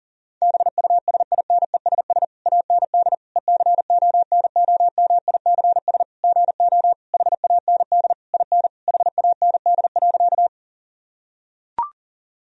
Morse code, heard as a keyed tone, about 37 wpm, 700 Hz. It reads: BUSINESS AND ECONOMICS GO HAND IN HAND.
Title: Sentences from Top 1000 Words - Review Part 3 - 40wpm